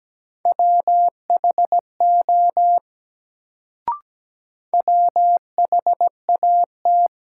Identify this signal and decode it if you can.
Morse code, heard as a keyed tone, about 17 wpm, 700 Hz. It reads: WHO WHAT
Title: Sentences from Top 100 Words - New Word Drill - 17wpm